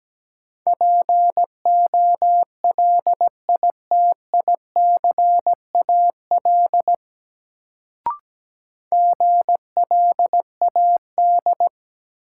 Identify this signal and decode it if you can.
Morse code, heard as a keyed tone, about 17 wpm, 700 Hz. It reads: POLITICAL GLAD
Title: Sentences from Top 800 Words - New Word Drill - 17wpm